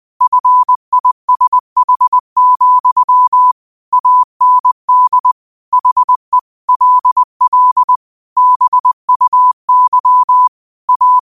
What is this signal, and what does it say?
Morse code, heard as a keyed tone, about 20 wpm, 1000 Hz. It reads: FISH, AND HE LL BUY A
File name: qotdfiles / 20wpm